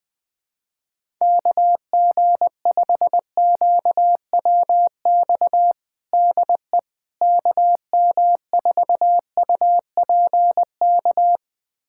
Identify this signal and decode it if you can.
Morse code, heard as a keyed tone, about 20 wpm, 700 Hz. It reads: KG5QWX DE KM4UPK
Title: USA Callsign DE Callsign - 6 Characters - 20wpm